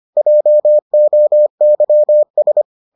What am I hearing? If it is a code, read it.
Morse code, heard as a keyed tone, about 25 wpm, 600 Hz. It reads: JOYS